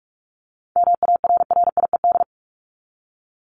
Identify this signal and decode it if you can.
Morse code, heard as a keyed tone, about 44 wpm, 700 Hz. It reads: MARRIED
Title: ED Suffix - Top 100 Words with Base Words 45wpm